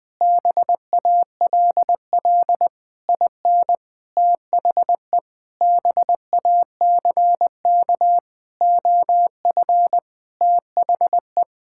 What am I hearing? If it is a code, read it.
Morse code, heard as a keyed tone, about 20 wpm, 700 Hz. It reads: BALL IN THE BACK OF THE
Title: Sentences with 8-Word Phrases - 20wpm